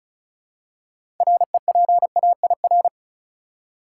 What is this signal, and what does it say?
Morse code, heard as a keyed tone, about 36 wpm, 700 Hz. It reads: REPAIR
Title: Six Letter Words - 35wpm